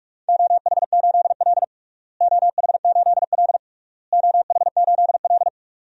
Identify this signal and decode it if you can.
Morse code, heard as a keyed tone, about 48 wpm, 700 Hz. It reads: OH8L OH8L OH8L
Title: DXCC-100 Call Signs - 4 Characters - Repeated 3x - 45wpm